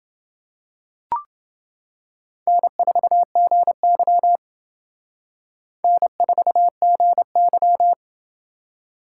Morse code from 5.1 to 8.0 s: N4GY

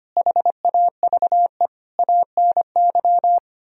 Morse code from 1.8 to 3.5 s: ANY

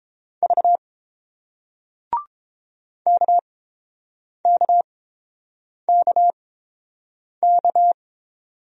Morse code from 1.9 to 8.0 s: KKKK

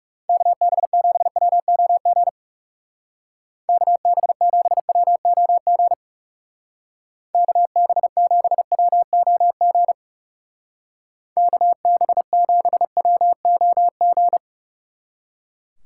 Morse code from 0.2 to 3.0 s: KB7WOG

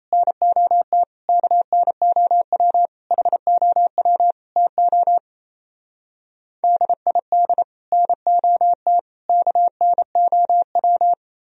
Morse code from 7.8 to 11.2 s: NOT KNOW

